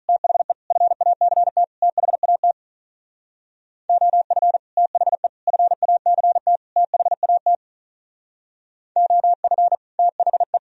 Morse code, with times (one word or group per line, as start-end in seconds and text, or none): 0.1-0.5 THE
0.7-1.6 FACT
1.8-2.5 THAT
3.9-4.6 OF
4.8-5.3 THE
5.5-6.6 FACT
6.8-7.5 THAT
9.0-9.7 OF
10.0-10.6 THE